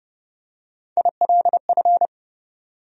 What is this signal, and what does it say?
Morse code, heard as a keyed tone, about 29 wpm, 700 Hz. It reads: ILF
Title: USA Call Sign Postfixes - 3 Letters 30wpm